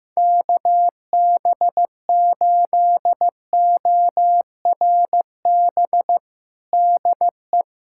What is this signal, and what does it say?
Morse code, heard as a keyed tone, about 15 wpm, 700 Hz. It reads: KB8ORB DE
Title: USA Callsign DE Callsign - 6 Characters - 15wpm